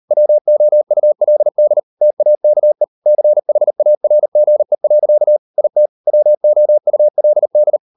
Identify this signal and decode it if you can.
Morse code, heard as a keyed tone, about 38 wpm, 600 Hz. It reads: WOULD TAKE CHARGE. IT WOULD